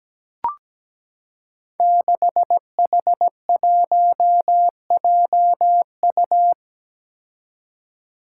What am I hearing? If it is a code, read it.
Morse code, heard as a keyed tone, about 17 wpm, 700 Hz. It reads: 6H1JU
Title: Five Challenging Letter-Number Combinations 17wpm